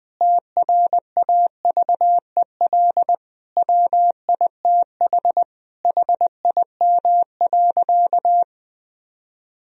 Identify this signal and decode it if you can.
Morse code, heard as a keyed tone, about 20 wpm, 700 Hz. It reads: TRAVEL WITH HIM.
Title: Sentences from Top 1000 Words - Review Part 2 - 20wpm